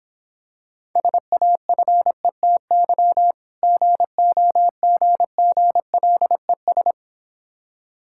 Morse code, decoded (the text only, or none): SAFETY GOGGLES